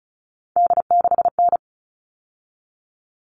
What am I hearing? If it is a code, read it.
Morse code, heard as a keyed tone, about 34 wpm, 700 Hz. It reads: D6N
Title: All Cut Numbers - 3-Digit 35wpm